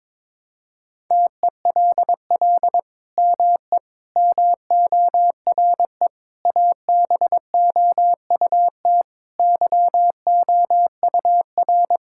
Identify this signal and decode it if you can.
Morse code, heard as a keyed tone, about 21 wpm, 700 Hz. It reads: TELL ME MORE ABOUT YOUR
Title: Sentences from Top 1000 Words - Review Part 5 - 22wpm